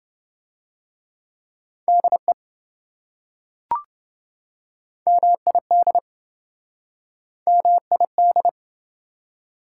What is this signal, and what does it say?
Morse code, heard as a keyed tone, about 25 wpm, 700 Hz. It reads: DE MID MID
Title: Speed-Racing Prefixes and Suffixes 20wpm